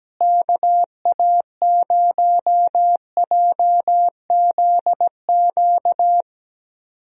Morse code, heard as a keyed tone, about 17 wpm, 700 Hz. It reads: KA0JZQ
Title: USA Call Signs - 6 Characters - Repeated 3x - 17wpm